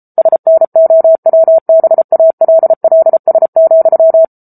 Morse code, heard as a keyed tone, about 34 wpm, 650 Hz. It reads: SNOWBALLS,
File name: Book Two / 07-Chapter 7 - 33 WPM